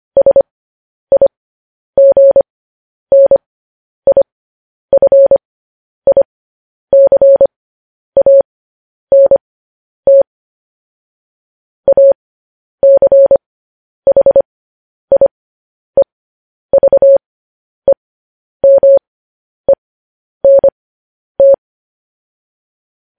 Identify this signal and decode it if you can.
Morse code, heard as a keyed tone, about 25 wpm, spaced as for 10 wpm, 550 Hz. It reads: SIGNIFICANT ACHIEVEMENT
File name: podcast / News headlines-1x-25x10